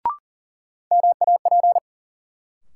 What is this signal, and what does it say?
Morse code, heard as a keyed tone, about 40 wpm, 700 Hz. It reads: MAP
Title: Sentences from Top 400 Words - New Word Drill - 40wpm